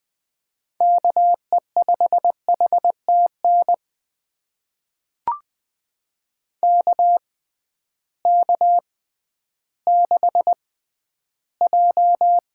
Morse code, heard as a keyed tone, about 20 wpm, spaced as for 12 wpm, 700 Hz. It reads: KE5HTN KK6J
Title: USA Call Signs - 6x Character Spacing - 20wpm